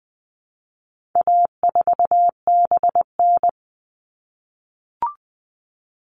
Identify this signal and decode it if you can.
Morse code, heard as a keyed tone, about 20 wpm, 700 Hz. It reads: A4BN
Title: All Cut Numbers - 4-Digit 20wpm